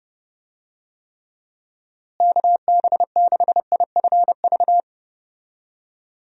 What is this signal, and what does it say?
Morse code, heard as a keyed tone, about 31 wpm, 700 Hz. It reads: KB6IFV
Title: USA Call Signs - 6 Characters - Repeated 3x - 30wpm